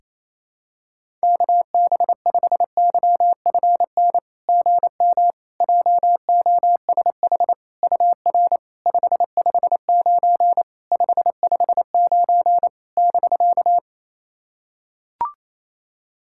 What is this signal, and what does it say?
Morse code, heard as a keyed tone, about 28 wpm, 700 Hz. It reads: KB5YFN GM JOSH UR 559 559 <BK>
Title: SOTA QSO Practice 28wpm